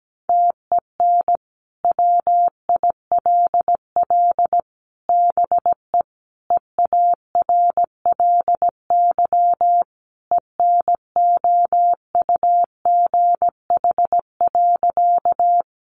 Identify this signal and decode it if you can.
Morse code, heard as a keyed tone, about 17 wpm, 700 Hz. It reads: TEN WILL BE EARLY ENOUGH.
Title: Sentences from Top 400 Words 17wpm